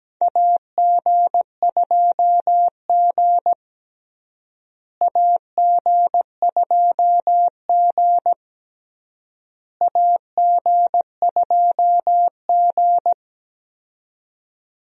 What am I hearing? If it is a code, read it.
Morse code, heard as a keyed tone, about 17 wpm, 700 Hz. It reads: AG2G AG2G AG2G
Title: USA Call Signs - 4 Characters - Repeated 3x - 17wpm